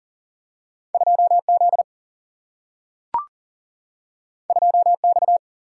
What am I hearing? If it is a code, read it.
Morse code, heard as a keyed tone, about 40 wpm, 700 Hz. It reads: 2Z 2X
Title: Single-Letter Single-Number 40wpm